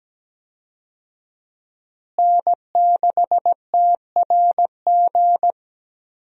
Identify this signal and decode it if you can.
Morse code, heard as a keyed tone, about 17 wpm, 700 Hz. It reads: N6TRG
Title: USA Call Signs - 5 Characters - Repeated 3x - 17wpm